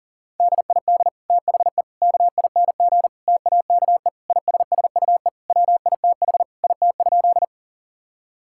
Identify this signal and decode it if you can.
Morse code, heard as a keyed tone, about 40 wpm, 700 Hz. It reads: DID THE KING TAKE ISSUE WITH IT?
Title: Sentences from Top 1000 Words - Review Part 6 - 40wpm